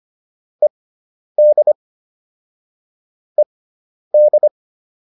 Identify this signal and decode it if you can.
Morse code, heard as a keyed tone, about 25 wpm, spaced as for 10 wpm, 600 Hz. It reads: ED ED